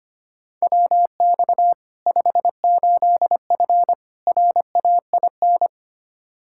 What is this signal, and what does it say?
Morse code, heard as a keyed tone, about 25 wpm, 700 Hz. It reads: WX 58F RAIN
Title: Practice QSO - Exchange 2 - Info - 25wpm